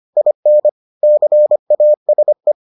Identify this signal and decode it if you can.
Morse code, heard as a keyed tone, about 25 wpm, 600 Hz. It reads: IN CASE